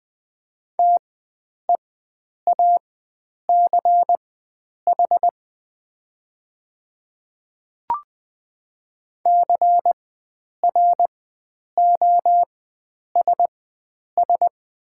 Morse code, heard as a keyed tone, about 20 wpm, spaced as for 9 wpm, 700 Hz. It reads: TEACH CROSS
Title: 4x Character Spacing - Top 500 Words - 20wpm